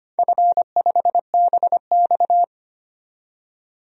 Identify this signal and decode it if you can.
Morse code, heard as a keyed tone, about 25 wpm, 700 Hz. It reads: F5BX
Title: DXCC-100 Call Signs - 4 Characters - 25wpm